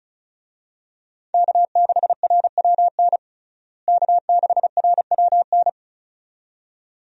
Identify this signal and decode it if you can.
Morse code, heard as a keyed tone, about 34 wpm, 700 Hz. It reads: K6RWN K6RWN
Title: USA Call Signs - 5 Characters - Repeated 2x - 35wpm